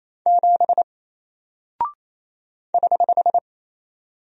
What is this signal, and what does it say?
Morse code, heard as a keyed tone, about 28 wpm, 700 Hz. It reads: 7 <HH>
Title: Single Character (Letter:Number:Punctuation:Prosign) Alt 28wpm